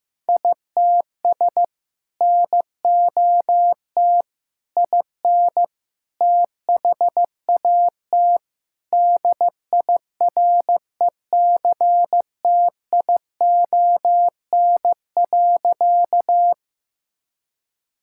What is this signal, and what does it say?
Morse code, heard as a keyed tone, about 15 wpm, 700 Hz. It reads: ITS NOT IN THAT DIRECTION.
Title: Sentences from Top 1000 Words - Review Part 5 - 15wpm